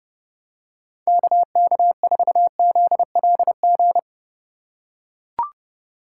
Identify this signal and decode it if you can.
Morse code, heard as a keyed tone, about 30 wpm, 700 Hz. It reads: KK4ZLG